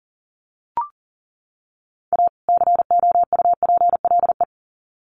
Morse code, heard as a keyed tone, about 37 wpm, 700 Hz. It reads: A COUPLE
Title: Sets of 2 Words 40wpm